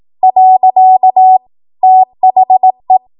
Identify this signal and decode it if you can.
Morse code, heard as a keyed tone, about 18 wpm, 750 Hz. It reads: . THE